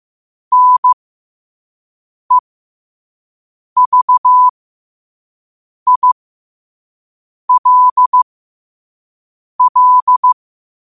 Morse code, heard as a keyed tone, about 15 wpm, spaced as for 5 wpm, 1000 Hz. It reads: NEVILL
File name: qotdfiles / 5wpm